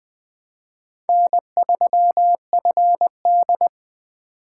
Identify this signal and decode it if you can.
Morse code, heard as a keyed tone, about 20 wpm, 700 Hz. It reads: N3FD
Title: Speed Racing - USA 1x2 Call Signs 17wpm